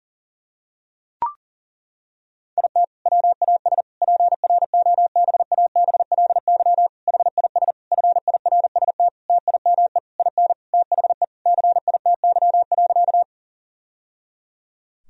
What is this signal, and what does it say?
Morse code, heard as a keyed tone, about 37 wpm, 700 Hz. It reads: IT WAS PROBABLY HIS FIRST TIME IN THE CITY.
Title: Sentences from Top 1000 Words - Review Part 4 - 40wpm